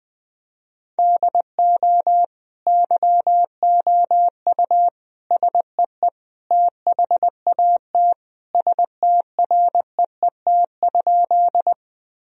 Morse code, heard as a keyed tone, about 20 wpm, 700 Hz. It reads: DO YOU SEE THAT STREET?
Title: Sentences from Top 1000 Words - Review Part 3 - 20wpm